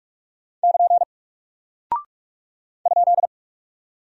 Morse code, heard as a keyed tone, about 46 wpm, 700 Hz. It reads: <KN> ?
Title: Single Character (Letter:Number:Punctuation:Prosign) 45wpm